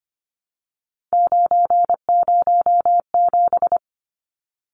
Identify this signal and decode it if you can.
Morse code, heard as a keyed tone, about 25 wpm, 700 Hz. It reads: 907
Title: Numbers - 3-Digit 25wpm